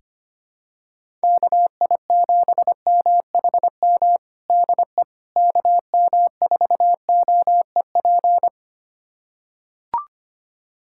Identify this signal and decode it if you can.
Morse code, heard as a keyed tone, about 25 wpm, 700 Hz. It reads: KI7MHM DE KM4OEP
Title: USA Callsign DE Callsign - 6 Characters - 25wpm